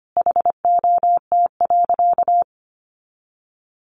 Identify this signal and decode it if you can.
Morse code, heard as a keyed tone, about 25 wpm, 700 Hz. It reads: HOT.
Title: Additional Easy Phrases 25wpm